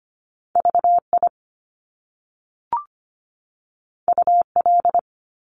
Morse code, heard as a keyed tone, about 25 wpm, 700 Hz. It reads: VI UL
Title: Top Two Letter Combinations in Words 25wpm